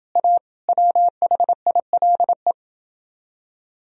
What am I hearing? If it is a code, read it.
Morse code, heard as a keyed tone, about 29 wpm, 700 Hz. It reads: A WHILE